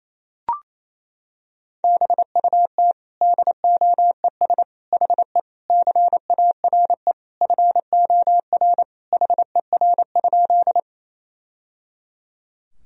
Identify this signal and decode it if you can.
Morse code, heard as a keyed tone, about 28 wpm, 700 Hz. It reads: BUT DOES HE CARE FOR HER?
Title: Sentences from Top 1000 Words - Review Part 1 - 28wpm